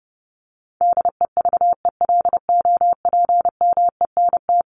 Morse code, heard as a keyed tone, about 30 wpm, 700 Hz. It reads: DEVELOPMENT
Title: Top 501-750 Words 30wpm